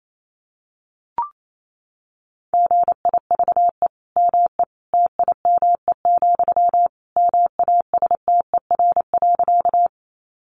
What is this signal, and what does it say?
Morse code, heard as a keyed tone, about 28 wpm, 700 Hz. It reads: GIVE ME TIME, MASTER.
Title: Sentences from Top 1300 Words 28wpm